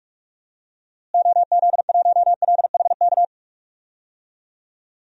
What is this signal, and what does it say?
Morse code, heard as a keyed tone, about 46 wpm, 700 Hz. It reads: OZ1LHK
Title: DXCC-100 Call Signs - 6 Characters - Repeated 2x - 45wpm